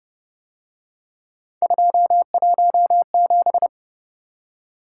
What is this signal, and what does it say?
Morse code, heard as a keyed tone, about 26 wpm, 700 Hz. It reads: 217